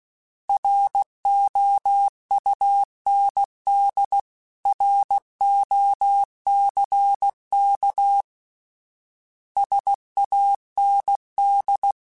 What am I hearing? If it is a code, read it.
Morse code, heard as a keyed tone, about 16 wpm, 800 Hz. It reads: ROUND ROCK SAND
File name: fromSMs / FromSMs mp3s / Cities@15